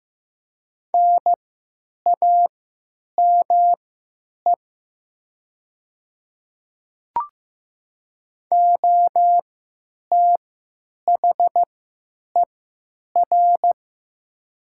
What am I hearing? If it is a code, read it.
Morse code, heard as a keyed tone, about 15 wpm, spaced as for 9 wpm, 700 Hz. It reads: NAME OTHER
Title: 3x Character Spacing - Top 200 Words - 15wpm